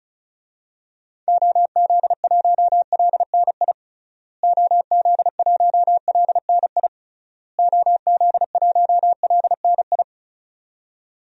Morse code, heard as a keyed tone, about 36 wpm, 700 Hz. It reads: OZ1LNI OZ1LNI OZ1LNI
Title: DXCC-100 Call Signs - 6 Characters - Repeated 3x - 35wpm